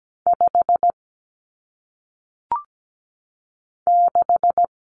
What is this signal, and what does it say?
Morse code, heard as a keyed tone, about 17 wpm, 700 Hz. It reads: 5 6
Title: Numbers - 1-Digit 17wpm